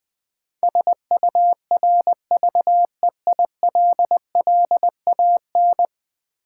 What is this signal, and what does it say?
Morse code, heard as a keyed tone, about 20 wpm, 700 Hz. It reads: SURVEILLAN
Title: Build A Word - 5 Letters At A Time - 12-Letter Words - 20wpm